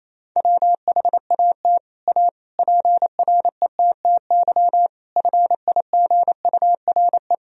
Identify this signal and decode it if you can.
Morse code, heard as a keyed tone, about 28 wpm, 700 Hz. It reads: WHAT A PRETTY FIGURE
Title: Sentences from Top 1000 Words - Review Part 2 - 28wpm